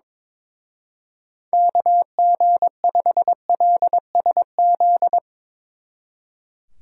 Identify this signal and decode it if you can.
Morse code, heard as a keyed tone, about 24 wpm, 700 Hz. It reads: KG5LSZ
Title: Speed Racing - USA 2x3 Call Signs 22wpm